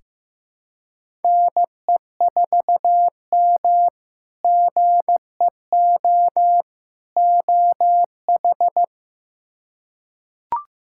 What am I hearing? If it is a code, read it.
Morse code, heard as a keyed tone, about 16 wpm, 700 Hz. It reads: NE4M GEO OH
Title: K1USN SST QSO Practice - 15wpm